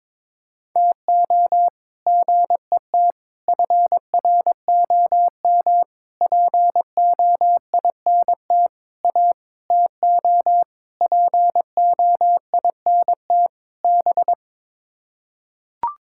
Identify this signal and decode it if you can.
Morse code, heard as a keyed tone, about 21 wpm, 700 Hz. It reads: TO GET FROM POINT A TO POINT B
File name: practice / 22wpm / Sets of 8 Words - 22wpm